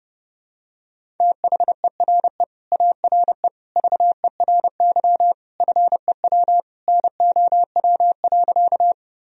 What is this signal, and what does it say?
Morse code, heard as a keyed tone, about 31 wpm, 700 Hz. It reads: THERE ARE VERY FEW NOW.
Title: Sentences from Top 1000 Words - Review Part 1 - 30wpm